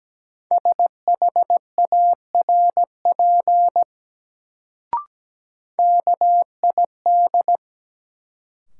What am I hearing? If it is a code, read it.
Morse code, heard as a keyed tone, about 17 wpm, 700 Hz. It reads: SHARP KID
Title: Sentences from Top 1600 Words - New Word Drill Alt - 17wpm